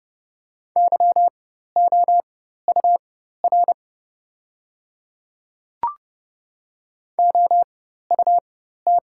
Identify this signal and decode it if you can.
Morse code, heard as a keyed tone, about 30 wpm, spaced as for 14 wpm, 700 Hz. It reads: YOUR OUT
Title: 4x Character Spacing - Top 100 Words - 30wpm